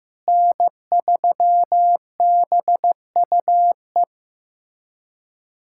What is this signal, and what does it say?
Morse code, heard as a keyed tone, about 15 wpm, 700 Hz. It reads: N3BUE
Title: USA Callsign DE Callsign - 5 Characters - 15wpm